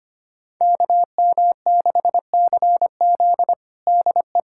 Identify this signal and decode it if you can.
Morse code, heard as a keyed tone, about 25 wpm, 700 Hz. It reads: KM6CZ DE
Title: USA Callsign DE Callsign - 5 Characters - 25wpm